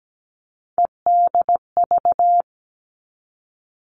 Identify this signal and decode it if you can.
Morse code, heard as a keyed tone, about 17 wpm, 700 Hz. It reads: EDV
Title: All Cut Numbers - 3-Digit 17wpm